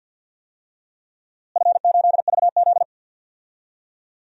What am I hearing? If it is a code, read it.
Morse code, heard as a keyed tone, about 53 wpm, 700 Hz. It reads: F8VZ
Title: DXCC-100 Call Signs - 4 Characters - Repeated 2x - 50wpm